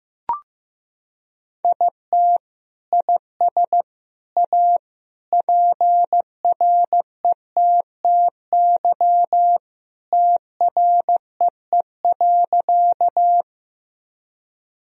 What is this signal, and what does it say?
Morse code, heard as a keyed tone, about 15 wpm, 700 Hz. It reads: IT IS A PRETTY TREE.
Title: Sentences from Top 1000 Words - Review Part 1 - 15wpm